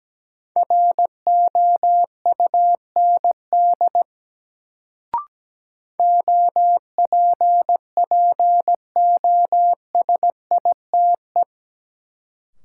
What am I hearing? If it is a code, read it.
Morse code, heard as a keyed tone, about 17 wpm, 700 Hz. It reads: ROUND OPPOSITE
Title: Sets of 5 Words Encore - New Word Drill - 17wpm